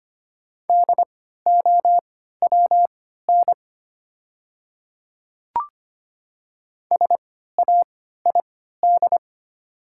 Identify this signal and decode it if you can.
Morse code, heard as a keyed tone, about 25 wpm, spaced as for 14 wpm, 700 Hz. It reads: DOWN SAID
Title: 3x Character Spacing - Top 200 Words - 25wpm